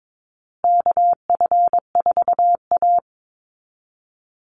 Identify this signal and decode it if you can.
Morse code, heard as a keyed tone, about 22 wpm, 700 Hz. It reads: KF4A
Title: Call Signs Repeated 2x - 22wpm